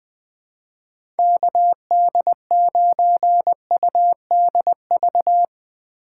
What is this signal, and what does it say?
Morse code, heard as a keyed tone, about 21 wpm, 700 Hz. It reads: KD9UDV
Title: 2023 POTA Activator Call Signs - 20wpm